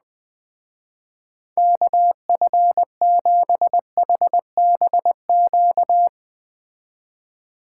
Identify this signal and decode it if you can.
Morse code, heard as a keyed tone, about 20 wpm, 700 Hz. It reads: KF7HBQ
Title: Speed Racing - USA 2x3 Call Signs 17wpm